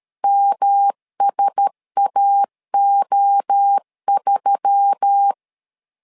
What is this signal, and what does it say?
Morse code, heard as a keyed tone, about 13 wpm, 800 Hz. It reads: MSAO3
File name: LEZIONI / 4 Q1S3 12wpm 2023